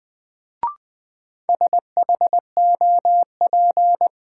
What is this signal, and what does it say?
Morse code, heard as a keyed tone, about 20 wpm, 700 Hz. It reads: SHOP
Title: Sentences from Top 1600 Words - New Word Drill - 20wpm